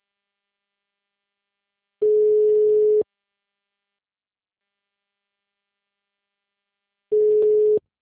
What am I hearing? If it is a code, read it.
Ringback tone (Europe, 425 Hz).